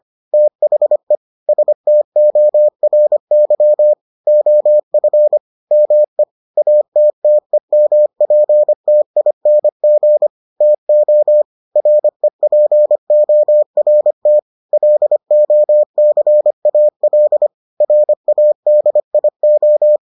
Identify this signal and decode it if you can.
Morse code, heard as a keyed tone, about 25 wpm, 600 Hz. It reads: THE STORY OF ME ATTEMPTING TO REPORT LOCAL RADIO